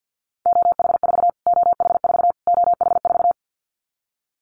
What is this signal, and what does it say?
Morse code, heard as a keyed tone, about 50 wpm, 700 Hz. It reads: OH4 OH4 OH4
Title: Callsign Prefixes for Top 25 DXCC - Repeated 3x - 50wpm